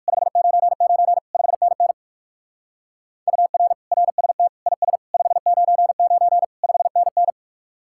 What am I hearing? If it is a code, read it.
Morse code, heard as a keyed tone, about 54 wpm, 700 Hz. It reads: 599 5NN UR RST IS 599 5NN
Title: Speed-Racing Common QSO Phrases Encore - 45wpm